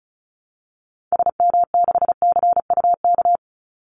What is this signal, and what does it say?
Morse code, heard as a keyed tone, about 34 wpm, 700 Hz. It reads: SM6CUK
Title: Call Signs Repeated 2x - 35wpm